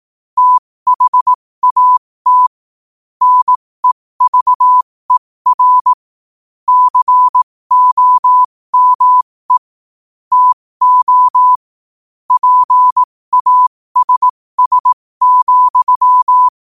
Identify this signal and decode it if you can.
Morse code, heard as a keyed tone, about 18 wpm, spaced as for 15 wpm, 1000 Hz. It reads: THAT NEVER COME TO PASS,
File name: qotdfiles / 15wpm